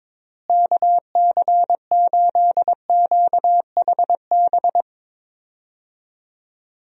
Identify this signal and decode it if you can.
Morse code, heard as a keyed tone, about 21 wpm, 700 Hz. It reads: KC8QHB
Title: USA Call Signs - 6 Characters - Repeated 3x - 22wpm